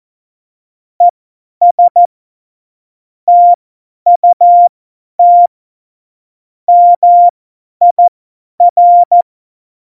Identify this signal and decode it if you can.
Morse code, heard as a keyed tone, about 14 wpm, spaced as for 10 wpm, 700 Hz. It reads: ES TUT MIR
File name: Morsekurs / Witze / Witz 06. Oktober 2020, 50 F70